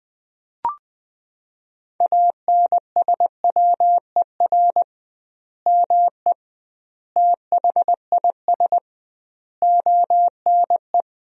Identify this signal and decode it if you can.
Morse code, heard as a keyed tone, about 21 wpm, 700 Hz. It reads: ANSWER ME THIS ONE
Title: 2x Word-Spacing - Sentences from Top 400 Words - 20wpm